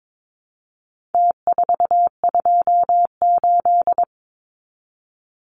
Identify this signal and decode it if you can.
Morse code, heard as a keyed tone, about 22 wpm, 700 Hz. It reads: T428
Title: Numbers - Contest Style - 4-Digit 22wpm